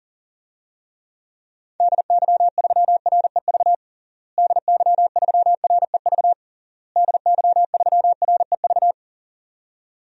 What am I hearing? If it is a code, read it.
Morse code, heard as a keyed tone, about 40 wpm, 700 Hz. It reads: DY3REV DY3REV DY3REV
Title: DXCC-100 Call Signs - 6 Characters - Repeated 3x - 40wpm